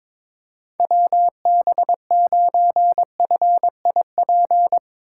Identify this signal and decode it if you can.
Morse code, heard as a keyed tone, about 22 wpm, 700 Hz. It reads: WB9FIP
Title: K1USN SST QSO Practice - 22wpm